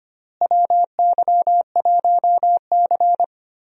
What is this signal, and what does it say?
Morse code, heard as a keyed tone, about 25 wpm, 700 Hz. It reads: WY1C
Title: USA Call Signs - 4 Characters - Repeated 3x - 25wpm